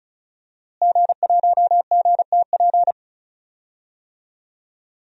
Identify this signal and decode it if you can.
Morse code, heard as a keyed tone, about 36 wpm, 690 Hz. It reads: G1GTP
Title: DXCC-100 Call Signs - 5 Characters - Repeated 3x - 35wpm